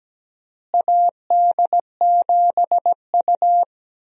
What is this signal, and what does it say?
Morse code, heard as a keyed tone, about 17 wpm, 700 Hz. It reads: AD7U
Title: USA Call Signs - 4 Characters - Repeated 3x - 17wpm